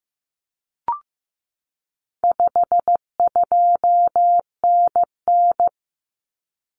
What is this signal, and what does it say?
Morse code, heard as a keyed tone, about 15 wpm, 700 Hz. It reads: 52NN
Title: Common Cut Numbers - 4-Digit 15wpm